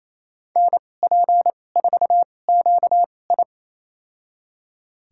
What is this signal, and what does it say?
Morse code, heard as a keyed tone, about 28 wpm, spaced as for 20 wpm, 700 Hz. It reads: NP4QI
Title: USA Call Signs - 2x Character Spacing - 28wpm